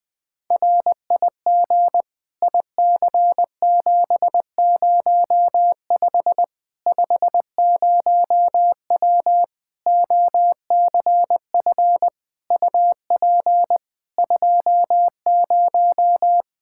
Morse code, heard as a keyed tone, about 21 wpm, 700 Hz. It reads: RIG IC705 50W OCF UP 20
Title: Practice QSO - Exchange 2 - Info - 20wpm